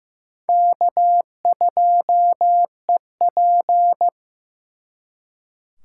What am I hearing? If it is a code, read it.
Morse code, heard as a keyed tone, about 18 wpm, 700 Hz. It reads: K2EP
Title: Speed Racing - USA 1x2 Call Signs 15wpm